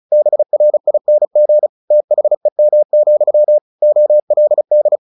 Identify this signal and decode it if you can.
Morse code, heard as a keyed tone, about 35 wpm, 600 Hz. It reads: BRING THEM, OLD
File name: the old man and the sea 2 35wpm